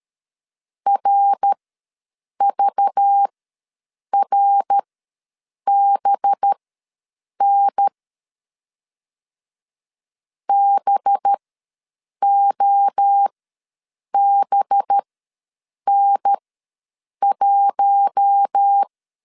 Morse code, heard as a keyed tone, about 13 wpm, spaced as for 7 wpm, 800 Hz. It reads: RVRBN BOBN1
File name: LEZIONI / 5 LBVC / 5 LBV ws 12wpm 3015